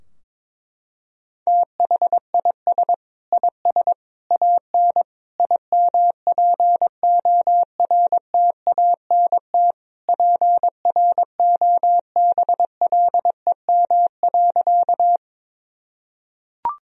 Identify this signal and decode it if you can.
Morse code, heard as a keyed tone, about 22 wpm, 700 Hz. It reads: THIS IS AN IMPORTANT PROBLEM.
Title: Sentences from Top 1000 Words - Review Part 2 - 22wpm